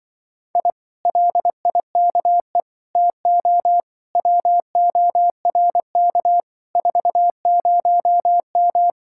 Morse code, heard as a keyed tone, about 23 wpm, 700 Hz. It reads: I LIKE TO WORK 40M
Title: Speed-Racing Common QSO Phrases - 20wpm